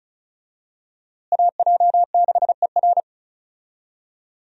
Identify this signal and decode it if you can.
Morse code, heard as a keyed tone, about 36 wpm, 700 Hz. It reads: AJ6ER
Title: USA Call Signs - 5 Characters - Repeated 3x - 35wpm